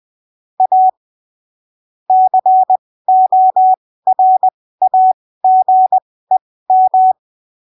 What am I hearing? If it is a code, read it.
Morse code, heard as a keyed tone, about 20 wpm, spaced as for 15 wpm, 750 Hz. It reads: A CORAGEM
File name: arquivos / texto / texto 19 20ppm